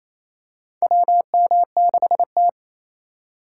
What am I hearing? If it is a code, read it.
Morse code, heard as a keyed tone, about 28 wpm, 700 Hz. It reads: WM6T
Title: USA Call Signs - 4 Characters - 2x Character Spacing - 28wpm